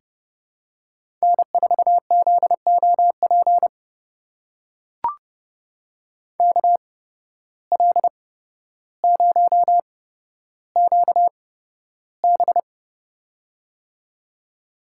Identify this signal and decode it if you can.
Morse code, heard as a keyed tone, about 30 wpm, spaced as for 13 wpm, 700 Hz. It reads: N4ZOP KL0QB
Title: USA Call Signs - 8x Character Spacing - 30wpm